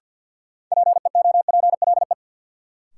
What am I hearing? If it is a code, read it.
Morse code, heard as a keyed tone, about 53 wpm, 700 Hz. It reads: PEOPLE
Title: Sentences from Top 200 Words - New Word Drill - 50wpm